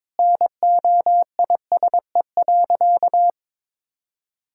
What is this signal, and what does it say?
Morse code, heard as a keyed tone, about 22 wpm, 700 Hz. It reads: NOISE.
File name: practice / 22wpm / Sentences with 2-Word Phrases - 22wpm